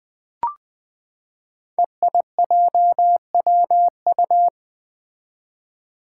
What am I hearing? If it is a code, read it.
Morse code, heard as a keyed tone, about 20 wpm, 700 Hz. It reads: EIJWU
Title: Five Challenging Letter-Number Combinations 20wpm